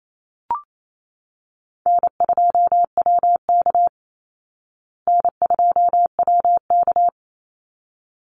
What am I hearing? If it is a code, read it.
Morse code, heard as a keyed tone, about 28 wpm, 700 Hz. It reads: N2WK N2WK
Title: Easy Call Signs Repeated 2x - 28wpm